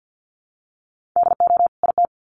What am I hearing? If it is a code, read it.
Morse code, heard as a keyed tone, about 50 wpm, 700 Hz. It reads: DO IT
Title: Sets of 2 Words 50wpm